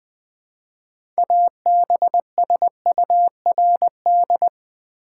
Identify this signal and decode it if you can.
Morse code, heard as a keyed tone, about 20 wpm, 700 Hz. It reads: ABSURD
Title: Build A Word - 2 Letters At A Time - 6-Letter Words - 20wpm